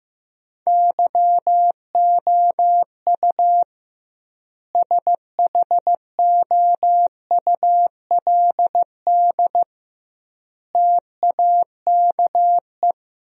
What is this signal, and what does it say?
Morse code, heard as a keyed tone, about 15 wpm, 700 Hz. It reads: YOU SHOULD TAKE
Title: 2x Word-Spacing - Sentences from Top 300 Words - 15wpm